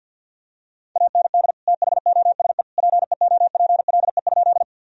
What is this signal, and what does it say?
Morse code, heard as a keyed tone, about 50 wpm, 700 Hz. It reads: AND THOSE PEOPLE?
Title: Sentences from Top 1000 Words - Review Part 1 - 50wpm